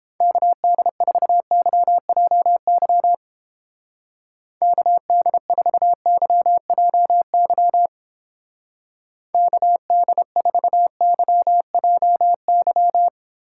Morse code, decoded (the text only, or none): KD4YJY KD4YJY KD4YJY